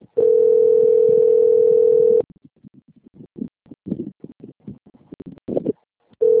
Ringback tone (North America), 440 + 480 Hz.
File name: out- - - - - .14062